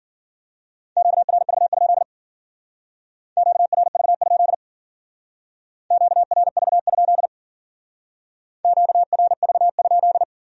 Morse code, heard as a keyed tone, about 53 wpm, 700 Hz. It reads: QRV? QRV? QRV? QRV?